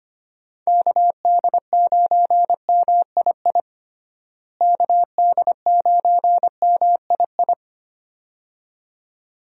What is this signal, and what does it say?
Morse code, heard as a keyed tone, about 25 wpm, 700 Hz. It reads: KD9MII KD9MII
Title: USA Call Signs - 6 Characters - Repeated 2x - 25wpm